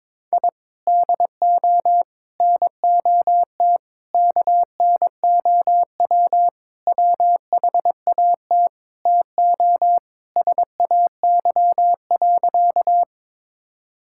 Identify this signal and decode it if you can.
Morse code, heard as a keyed tone, about 22 wpm, 700 Hz. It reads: I DO NOT KNOW WHAT TO SAY.
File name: practice / 22wpm / Sentences with 4-Word Phrases - 22wpm